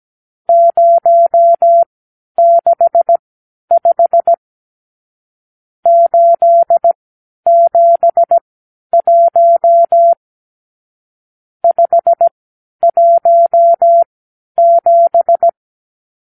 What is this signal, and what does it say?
Morse code, heard as a keyed tone, about 17 wpm, spaced as for 10 wpm, 680 Hz. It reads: 065 871 517
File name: arrl / Review (1-0)